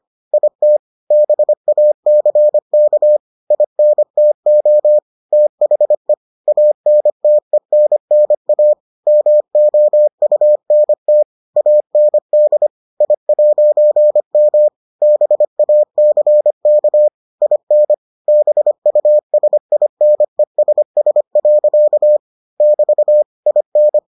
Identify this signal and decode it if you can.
Morse code, heard as a keyed tone, about 25 wpm, 600 Hz. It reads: IT BACK INTO THE ANTENNA MOUNT AND I'M BACK IN BUSINESS. <BT> IN